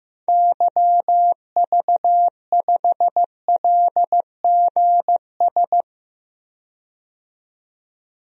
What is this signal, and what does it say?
Morse code, heard as a keyed tone, about 16 wpm, 700 Hz. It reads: YV5LGS
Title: DXCC-100 Call Signs - 6 Characters - 15wpm